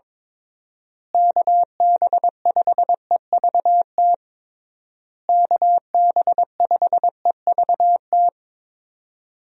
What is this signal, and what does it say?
Morse code, heard as a keyed tone, about 22 wpm, 700 Hz. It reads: KB5EVT KB5EVT
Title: USA Call Signs - 6 Characters - Repeated 3x - 22wpm